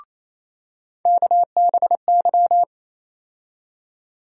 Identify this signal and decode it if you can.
Morse code, heard as a keyed tone, about 28 wpm, 700 Hz. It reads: KBY
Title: USA Call Sign Postfixes - 3 Letters 28wpm